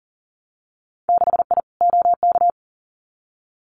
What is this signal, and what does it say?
Morse code, heard as a keyed tone, about 42 wpm, 700 Hz. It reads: 6I OK
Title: Winter Field Day Exchanges Encore 40wpm